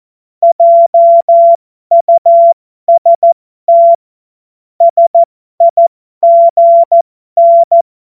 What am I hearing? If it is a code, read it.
Morse code, heard as a keyed tone, about 14 wpm, spaced as for 12 wpm, 680 Hz. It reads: JUST SIGN
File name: Morsekurs / Witze